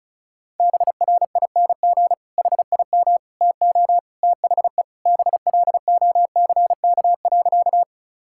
Morse code, heard as a keyed tone, about 34 wpm, 700 Hz. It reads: BRING HIM TO THE BLOCK.
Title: Sentences from Top 1600 Words 35wpm